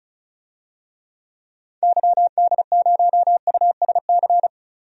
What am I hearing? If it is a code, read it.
Morse code, heard as a keyed tone, about 36 wpm, 700 Hz. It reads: YD0USC
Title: DXCC-100 Call Signs - 6 Characters - Repeated 2x - 35wpm